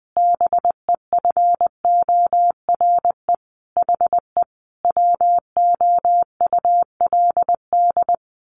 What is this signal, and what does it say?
Morse code, heard as a keyed tone, about 20 wpm, 700 Hz. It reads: BEFORE HE WOULD
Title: A Sound Of Thunder 20wpm